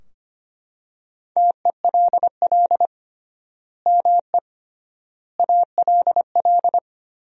Morse code, heard as a keyed tone, about 25 wpm, 700 Hz. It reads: TELL ME ALL
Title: 3x Word-Spacing - Sentences from Top 200 Words - 25wpm